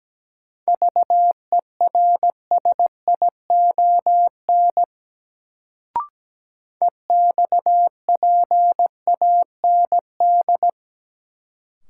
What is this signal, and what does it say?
Morse code, heard as a keyed tone, about 17 wpm, 700 Hz. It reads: VERSION EXPAND
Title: Sentences from Top 1400 Words - New Word Drill - 17wpm